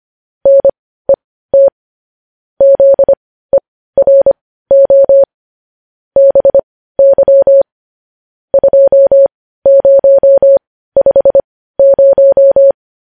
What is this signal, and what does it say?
Morse code, heard as a keyed tone, about 25 wpm, spaced as for 15 wpm, 550 Hz. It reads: NET ZERO BY 2050
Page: news headlines in morse code at 15 wpm